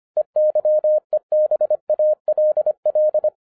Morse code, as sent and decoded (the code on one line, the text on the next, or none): . -.-- . -... .- .-.. .-..
EYEBALL